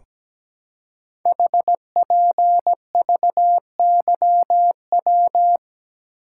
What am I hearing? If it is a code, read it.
Morse code, heard as a keyed tone, about 17 wpm, 700 Hz. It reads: HPVYW
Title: Code Groups - 5 Letters - 17wpm